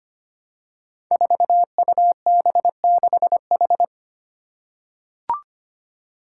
Morse code, heard as a keyed tone, about 25 wpm, 700 Hz. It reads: 4UB6H
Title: Five Challenging Letter-Number Combinations 25wpm